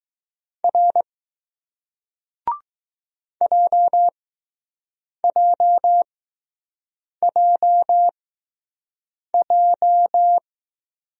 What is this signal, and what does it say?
Morse code, heard as a keyed tone, about 20 wpm, spaced as for 6 wpm, 700 Hz. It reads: R JJJJ